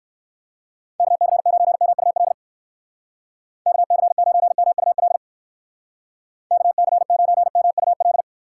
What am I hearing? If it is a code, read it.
Morse code, heard as a keyed tone, about 58 wpm, 700 Hz. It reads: KC9MUL KC9MUL KC9MUL